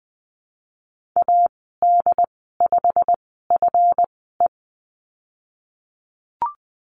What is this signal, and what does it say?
Morse code, heard as a keyed tone, about 20 wpm, spaced as for 15 wpm, 700 Hz. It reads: AD5FE
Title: 193 Lesson 31g - Letter v - Call Signs 20wpm